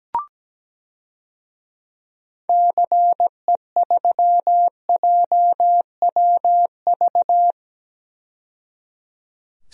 Morse code, heard as a keyed tone, about 17 wpm, 700 Hz. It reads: CE3JWV
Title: DXCC-100 Call Signs - 6 Characters - 17wpm